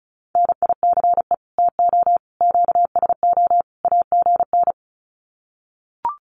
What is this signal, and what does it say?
Morse code, heard as a keyed tone, about 34 wpm, 700 Hz. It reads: NICE TO QSO AGN
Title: Common QSO Phrases 35wpm